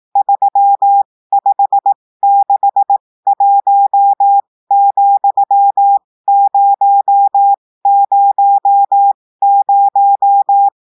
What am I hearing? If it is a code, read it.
Morse code, heard as a keyed tone, about 18 wpm, spaced as for 15 wpm, 800 Hz. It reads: 3561,000
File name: cours-du-lundi files / cours-1580-texte-chiffre-15-mpm